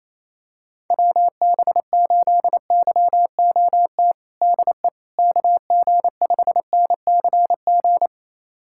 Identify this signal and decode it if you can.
Morse code, heard as a keyed tone, about 28 wpm, 700 Hz. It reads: WB8YOT DE KG5NCG
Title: USA Callsign DE Callsign 28wpm